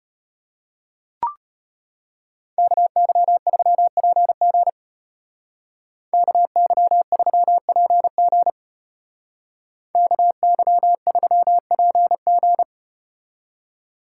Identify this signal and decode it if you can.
Morse code, heard as a keyed tone, about 34 wpm, 700 Hz. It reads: KY3PG KY3PG KY3PG